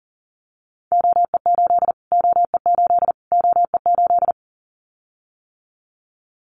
Morse code, heard as a keyed tone, about 40 wpm, 700 Hz. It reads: OE8 OE8 OE8
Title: Callsign Prefixes for Top 25 DXCC - Repeated 3x - 40wpm